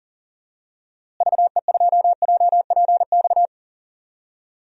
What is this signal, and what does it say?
Morse code, heard as a keyed tone, about 40 wpm, 700 Hz. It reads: VE2JPX
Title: DXCC-100 Call Signs - 6 Characters - Repeated 2x - 40wpm